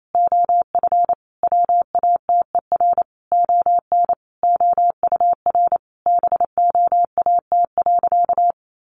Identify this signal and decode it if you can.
Morse code, heard as a keyed tone, about 28 wpm, 700 Hz. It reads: OF WATER ON OUR BOAT.
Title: Sentences from Top 1400 Words 28wpm